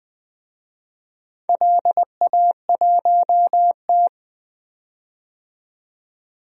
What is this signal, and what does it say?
Morse code, heard as a keyed tone, about 20 wpm, 700 Hz. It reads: LA1T
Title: DXCC-100 Call Signs - 4 Characters - 20wpm